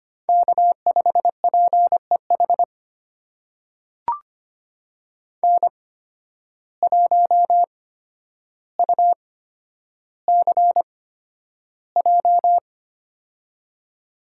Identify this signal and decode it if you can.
Morse code, heard as a keyed tone, about 25 wpm, spaced as for 11 wpm, 700 Hz. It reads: K5PEH N1UCJ
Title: USA Call Signs - 5 Characters - 8x Character Spacing - 25wpm